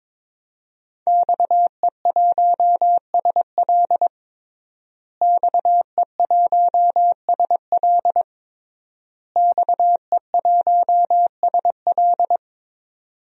Morse code, heard as a keyed tone, about 22 wpm, 700 Hz. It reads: XE1SL XE1SL XE1SL
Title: DXCC-100 Call Signs - 5 Characters - Repeated 3x - 22wpm